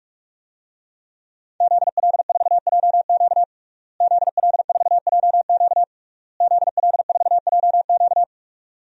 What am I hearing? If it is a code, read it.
Morse code, heard as a keyed tone, about 46 wpm, 700 Hz. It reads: ZL4JQ ZL4JQ ZL4JQ
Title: DXCC-100 Call Signs - 5 Characters - Repeated 3x - 45wpm